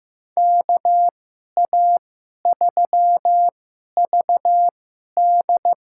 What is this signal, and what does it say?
Morse code, heard as a keyed tone, about 15 wpm, spaced as for 11 wpm, 700 Hz. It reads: KA3VD
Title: USA Call Signs - 2x Character Spacing - 15wpm